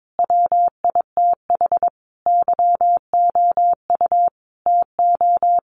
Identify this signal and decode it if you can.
Morse code, heard as a keyed tone, about 22 wpm, 700 Hz. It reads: WITH YOU TO
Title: Sentences from Top 1500 Words 22wpm